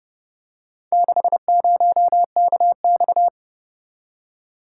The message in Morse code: 60KX

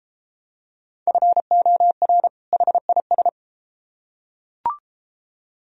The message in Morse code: FOR HIS

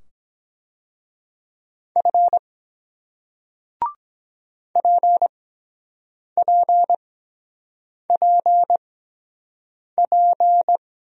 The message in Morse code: F PPPP

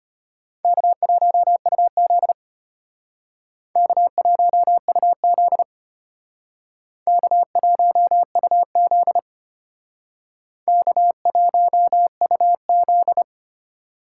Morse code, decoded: K1UZ K1UZ K1UZ K1UZ

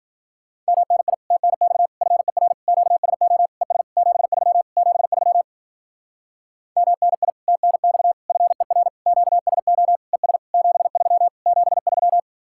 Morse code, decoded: MNI TNX FER QSO ES 73 73 MNI TNX FER QSO ES 73 73

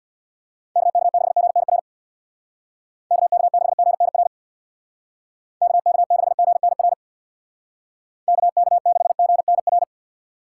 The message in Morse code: KK6GNR KK6GNR KK6GNR KK6GNR

